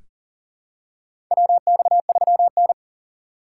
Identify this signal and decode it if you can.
Morse code, heard as a keyed tone, about 40 wpm, 690 Hz. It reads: WX3N
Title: USA Call Signs - 6x Character Spacing - 40wpm